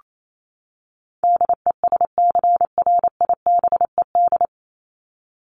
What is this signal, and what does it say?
Morse code, heard as a keyed tone, about 28 wpm, 700 Hz. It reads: DESCRIBED
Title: ED Suffix - Top 100 Words with Base Words 28wpm